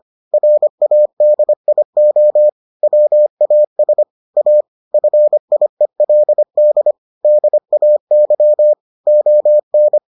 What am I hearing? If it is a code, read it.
Morse code, heard as a keyed tone, about 25 wpm, 600 Hz. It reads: RADIO WAS A FIELD DAY ON